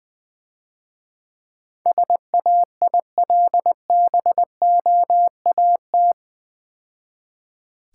Morse code, heard as a keyed tone, about 22 wpm, 700 Hz. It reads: SAILBOAT
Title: Easy Words - Speed-Racing with Hints - 20wpm